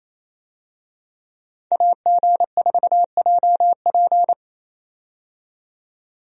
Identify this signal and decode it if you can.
Morse code, heard as a keyed tone, about 28 wpm, 700 Hz. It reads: AG4JP
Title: USA Call Signs - 5 Characters - 28wpm